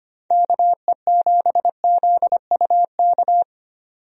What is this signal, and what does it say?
Morse code, heard as a keyed tone, about 25 wpm, 700 Hz. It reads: KE7ZUK
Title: USA Call Signs - 6 Characters - 25wpm